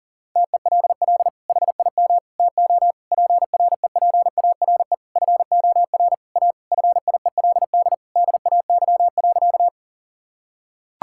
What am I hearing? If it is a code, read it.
Morse code, heard as a keyed tone, about 40 wpm, 690 Hz. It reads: TELL HIM TO PREPARE FOR A FIELD DAY.
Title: Sentences from Top 1000 Words - Review Part 6 - 40wpm